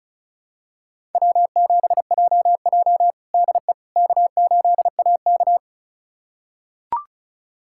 Morse code, heard as a keyed tone, about 36 wpm, 700 Hz. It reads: W7JJ DE K8AK
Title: USA Callsign DE Callsign - 4 Characters - 35wpm